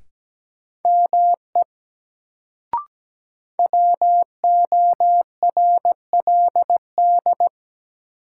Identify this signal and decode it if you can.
Morse code, heard as a keyed tone, about 17 wpm, 700 Hz. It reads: ME WORLD
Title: Sets of 2 Words - New Word Drill - 17wpm